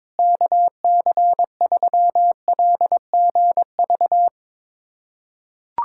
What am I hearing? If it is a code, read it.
Morse code, heard as a keyed tone, about 22 wpm, 700 Hz. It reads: KC3LGV
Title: USA Callsign DE Callsign - 6 Characters - 22wpm